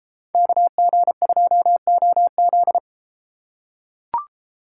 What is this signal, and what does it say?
Morse code, heard as a keyed tone, about 32 wpm, 700 Hz. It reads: KG2OZ